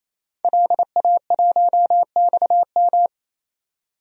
Morse code, heard as a keyed tone, about 28 wpm, 700 Hz. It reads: LA1XM
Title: DXCC-100 Call Signs - 5 Characters - 28wpm